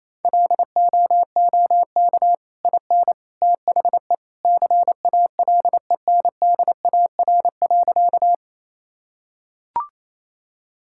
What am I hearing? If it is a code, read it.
Morse code, heard as a keyed tone, about 28 wpm, 700 Hz. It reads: LOOK IN THE CALENDAR.
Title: Sentences from Top 1600 Words 28wpm